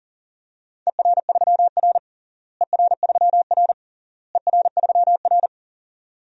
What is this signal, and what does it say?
Morse code, heard as a keyed tone, about 37 wpm, 700 Hz. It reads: ER3R ER3R ER3R
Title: DXCC-100 Call Signs - 4 Characters - Repeated 3x - 40wpm